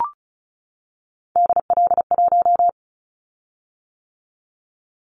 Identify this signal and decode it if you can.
Morse code, heard as a keyed tone, about 34 wpm, 700 Hz. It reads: DL1
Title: Callsign Prefixes for Top 25 DXCC - 35wpm